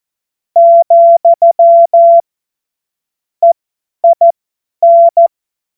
Morse code, heard as a keyed tone, about 14 wpm, spaced as for 10 wpm, 680 Hz. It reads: , EIN
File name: Morsekurs / Witze / Witz 23 Januar 2024 50 F70